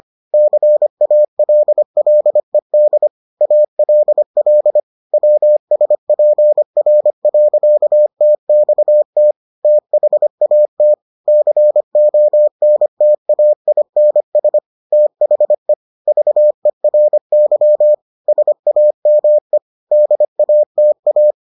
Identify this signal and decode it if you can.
Morse code, heard as a keyed tone, about 25 wpm, 600 Hz. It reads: CALLED ALL WSPR.TXT THAT CONTAINS THE VERY SAME DATA